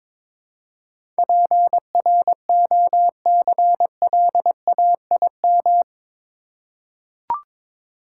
Morse code, heard as a keyed tone, about 22 wpm, 700 Hz. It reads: PROCLAIM
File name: practice / 22wpm / Eight Letter Words - 22wpm